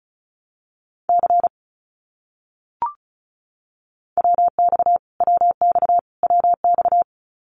Morse code, heard as a keyed tone, about 34 wpm, 700 Hz. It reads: C WX WX WX
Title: Abbreviations Repeated 3x - 35wpm